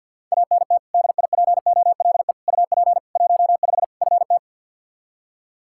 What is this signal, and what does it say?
Morse code, heard as a keyed tone, about 48 wpm, 700 Hz. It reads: ANT DIPOLE UP 15 FT
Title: Practice QSO - Exchange 2 - Info - 50wpm